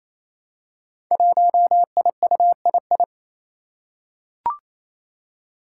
Morse code, heard as a keyed tone, about 27 wpm, 700 Hz. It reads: 1IUII
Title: Five Challenging Letter-Number Combinations 28wpm